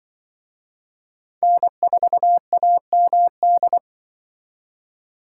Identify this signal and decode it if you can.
Morse code, heard as a keyed tone, about 24 wpm, 700 Hz. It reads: N4AMD
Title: Speed Racing - USA Call Signs 20wpm